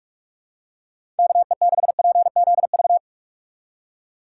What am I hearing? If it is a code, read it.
Morse code, heard as a keyed tone, about 46 wpm, 700 Hz. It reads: KE6PZV
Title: USA Call Signs - 6 Characters - 2x Character Spacing - 45wpm